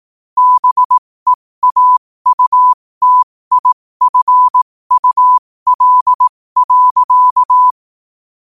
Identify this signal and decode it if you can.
Morse code, heard as a keyed tone, about 18 wpm, spaced as for 15 wpm, 1000 Hz. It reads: BEAUTIFUL.
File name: qotdfiles / 15wpm